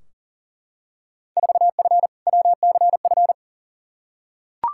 Morse code, heard as a keyed tone, about 40 wpm, 700 Hz. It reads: 4F WCF
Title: Field Day Exchanges - 40wpm